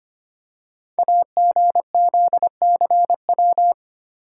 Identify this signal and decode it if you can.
Morse code, heard as a keyed tone, about 25 wpm, 700 Hz. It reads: AGZCW
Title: Code Groups - 5 Letters - 25wpm